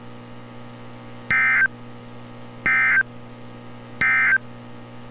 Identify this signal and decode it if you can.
Emergency Alert System (SAME) tones, heard as harsh buzzing bursts. It is heard from 1.2 to 4.5 s.
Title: NOAA Weather Radio - Altoona, PA